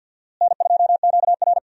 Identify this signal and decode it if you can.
Morse code, heard as a keyed tone, about 51 wpm, 700 Hz. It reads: N2QR
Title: USA Call Signs - 4 Characters - Repeated 3x - 50wpm